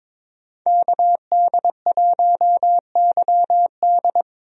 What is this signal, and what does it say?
Morse code, heard as a keyed tone, about 21 wpm, 700 Hz. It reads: KD1YD
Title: USA Call Signs - 5 Characters - 22wpm